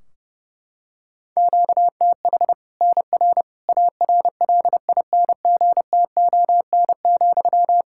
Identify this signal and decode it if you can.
Morse code, heard as a keyed tone, about 30 wpm, 700 Hz. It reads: QTH NR ARLINGTON,
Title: Practice QSO - Exchange 1 - Beginning - 30wpm